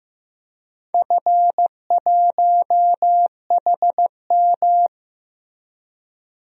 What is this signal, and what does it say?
Morse code, heard as a keyed tone, about 15 wpm, 700 Hz. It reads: F1HM
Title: DXCC-100 Call Signs - 4 Characters - Repeated 2x - 15wpm